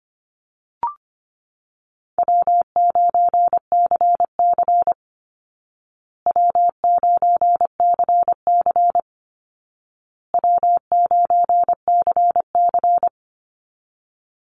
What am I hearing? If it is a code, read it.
Morse code, heard as a keyed tone, about 25 wpm, 700 Hz. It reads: W9CC W9CC W9CC
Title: Call Signs Repeated 2x - 25wpm